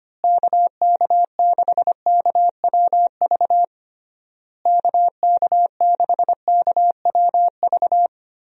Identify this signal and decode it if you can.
Morse code, heard as a keyed tone, about 25 wpm, 700 Hz. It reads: KK6KWV KK6KWV
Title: USA Call Signs - 6 Characters - Repeated 3x - 25wpm